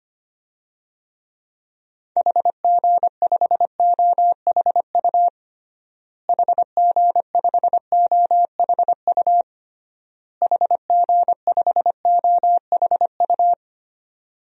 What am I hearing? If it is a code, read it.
Morse code, heard as a keyed tone, about 25 wpm, 690 Hz. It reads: HG5OHU HG5OHU HG5OHU
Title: DXCC-100 Call Signs - 6 Characters - Repeated 3x - 25wpm